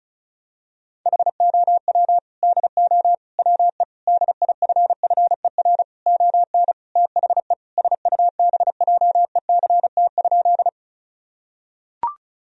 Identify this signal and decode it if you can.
Morse code, heard as a keyed tone, about 36 wpm, 700 Hz. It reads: HOW DO WE DIFFER ON THE SUBJECT?
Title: Sentences from Top 1000 Words - Review Part 1 - 35wpm